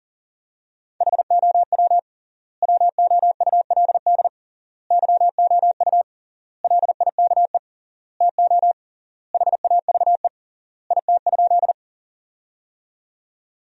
Morse code, heard as a keyed tone, about 40 wpm, 700 Hz. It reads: HOW WOULD YOU LIKE TO HAVE IT?
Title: 3x Word-Spacing - Sentences from Top 100 Words - 40wpm